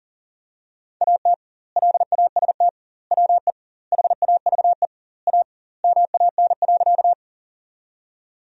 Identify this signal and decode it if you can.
Morse code, heard as a keyed tone, about 40 wpm, 700 Hz. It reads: AT LAST WE HAVE A MAN.
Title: 2x Word-Spacing - Sentences from Top 300 Words - 40wpm